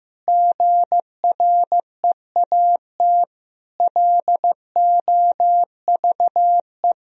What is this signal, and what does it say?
Morse code, heard as a keyed tone, about 15 wpm, 700 Hz. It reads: GREAT LOVE
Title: Sentences from Top 1000 Words - Review Part 2 - 15wpm